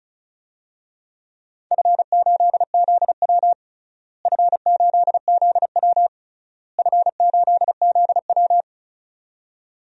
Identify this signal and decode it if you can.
Morse code, heard as a keyed tone, about 36 wpm, 700 Hz. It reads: F8ZW F8ZW F8ZW
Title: DXCC-100 Call Signs - 4 Characters - Repeated 3x - 35wpm